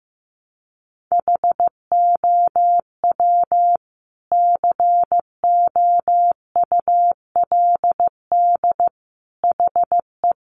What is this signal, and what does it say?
Morse code, heard as a keyed tone, about 15 wpm, 700 Hz. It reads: HOW COULD HE
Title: Sentences from Top 1300 Words 15wpm